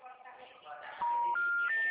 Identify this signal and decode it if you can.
Special information tone, with speech.
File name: out- -2009- - - .39819